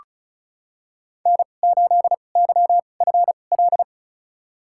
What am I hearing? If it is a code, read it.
Morse code, heard as a keyed tone, about 34 wpm, spaced as for 25 wpm, 700 Hz. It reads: N8YFL
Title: USA Call Signs - 5 Characters - 2x Character Spacing - 35wpm